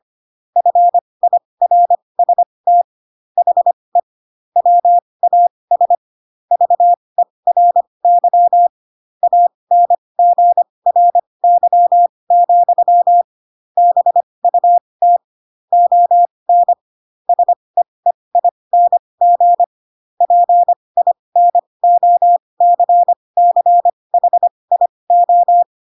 Morse code, heard as a keyed tone, about 25 wpm, spaced as for 20 wpm, 700 Hz. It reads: FIRST HE WAS VERY ANGRY, BUT ON SEEING PINOCCHIO
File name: Pinocchio / Pinocchio ch7 15 25 ik1hge